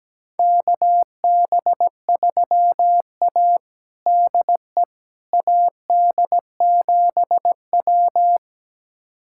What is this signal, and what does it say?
Morse code, heard as a keyed tone, about 17 wpm, 700 Hz. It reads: KB3A DE AD7W
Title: USA Callsign DE Callsign - 4 Characters - 17wpm